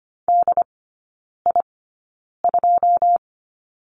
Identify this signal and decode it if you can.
Morse code, heard as a keyed tone, about 25 wpm, spaced as for 9 wpm, 690 Hz. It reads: DI2
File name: Lesson 14e - Number 2 - Three Characters 25wpm